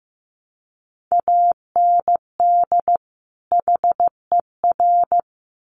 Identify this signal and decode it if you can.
Morse code, heard as a keyed tone, about 15 wpm, 700 Hz. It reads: AND HER
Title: Sets of 2 Words 15wpm